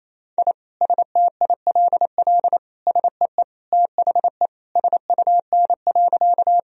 Morse code, heard as a keyed tone, about 28 wpm, 700 Hz. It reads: I STILL SEE THE SUN.
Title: Sentences from Top 1000 Words - Review Part 1 - 28wpm